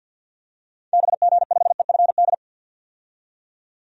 Morse code, heard as a keyed tone, about 52 wpm, 700 Hz. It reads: BG5EFD
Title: DXCC-100 Call Signs - 6 Characters - 50wpm